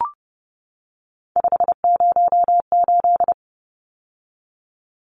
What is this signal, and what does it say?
Morse code, heard as a keyed tone, about 30 wpm, 700 Hz. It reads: 508
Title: Numbers - 3-Digit 30wpm